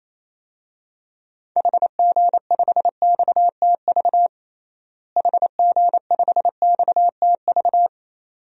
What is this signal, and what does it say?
Morse code, heard as a keyed tone, about 28 wpm, 700 Hz. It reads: HG5XTV HG5XTV
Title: DXCC-100 Call Signs - 6 Characters - Repeated 2x - 28wpm